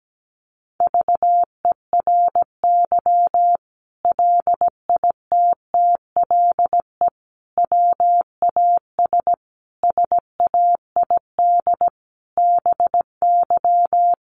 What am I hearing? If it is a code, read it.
Morse code, heard as a keyed tone, about 17 wpm, 700 Hz. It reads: VERY LITTLE WAS SAID BY
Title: Sentences from Top 600 Words 17wpm